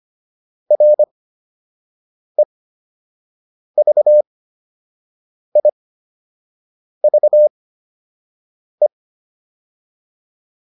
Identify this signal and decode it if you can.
Morse code, heard as a keyed tone, about 25 wpm, spaced as for 6 wpm, 600 Hz. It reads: REVIVE